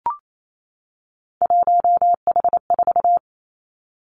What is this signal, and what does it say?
Morse code, heard as a keyed tone, about 28 wpm, 700 Hz. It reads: 1H4
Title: Three Challenging Letter-Number Combinations 28wpm